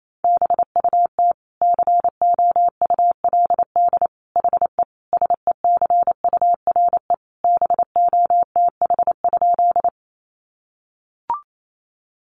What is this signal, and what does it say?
Morse code, heard as a keyed tone, about 28 wpm, 700 Hz. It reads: BUT COULD HE SECURE BOTH?
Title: Sentences from Top 1200 Words 28wpm